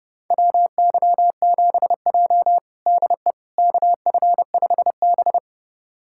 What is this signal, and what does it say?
Morse code, heard as a keyed tone, about 30 wpm, 700 Hz. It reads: WY7J DE KF5B
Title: USA Callsign DE Callsign - 4 Characters - 30wpm